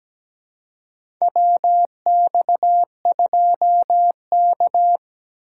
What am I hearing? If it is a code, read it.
Morse code, heard as a keyed tone, about 17 wpm, 700 Hz. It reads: WX2K
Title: USA Call Signs - 3x Character Spacing - 17wpm